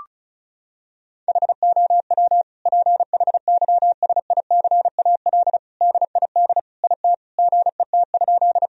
Morse code, heard as a keyed tone, about 34 wpm, 700 Hz. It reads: HOW PHYSICAL DID IT GET?
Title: Sentences from Top 1000 Words - Review Part 4 - 35wpm